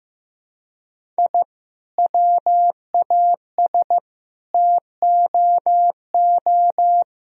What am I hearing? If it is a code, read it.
Morse code, heard as a keyed tone, about 15 wpm, 700 Hz. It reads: I WAS TOO
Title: Sentences from Top 1000 Words - Review Part 1 - 15wpm